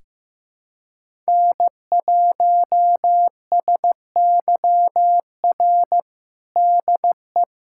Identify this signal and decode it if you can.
Morse code, heard as a keyed tone, about 15 wpm, 700 Hz. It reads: N1SYR DE
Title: USA Callsign DE Callsign - 5 Characters - 15wpm